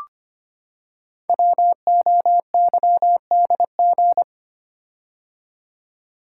Morse code, heard as a keyed tone, about 25 wpm, 700 Hz. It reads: WOYDG
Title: Code Groups - 5 Letters - 25wpm